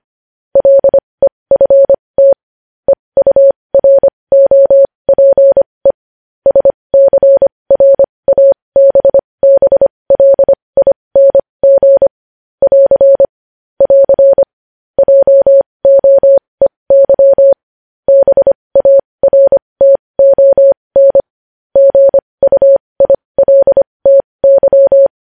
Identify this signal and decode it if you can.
Morse code, heard as a keyed tone, about 25 wpm, spaced as for 20 wpm, 550 Hz. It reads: LEFT EUROPE SCRABBLING <AR> <AR> JOEY BARTON GUILTY
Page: news-headlines-in-morse-code-at-20-wpm